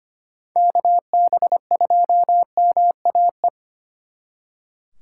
Morse code, heard as a keyed tone, about 33 wpm, 700 Hz. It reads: KB2MAE